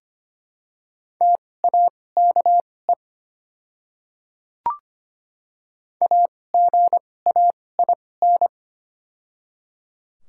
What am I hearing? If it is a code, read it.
Morse code, heard as a keyed tone, about 25 wpm, spaced as for 18 wpm, 700 Hz. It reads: TAKE AGAIN
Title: 2x Character Spacing - Top 300 Words - 25wpm